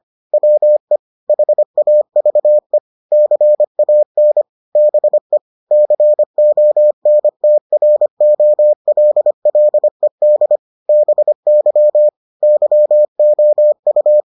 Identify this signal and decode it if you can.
Morse code, heard as a keyed tone, about 25 wpm, 600 Hz. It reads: WE HAVE CAN BE CONTROLLED BY YOU